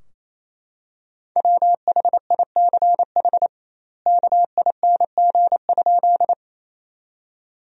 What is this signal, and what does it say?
Morse code, heard as a keyed tone, about 28 wpm, 700 Hz. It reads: WHICH KING?
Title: 2x Word-Spacing - Sentences from Top 400 Words - 28wpm